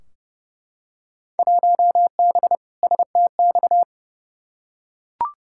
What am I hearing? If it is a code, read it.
Morse code, heard as a keyed tone, about 32 wpm, 700 Hz. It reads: 1B STX
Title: Field Day Exchanges - 30wpm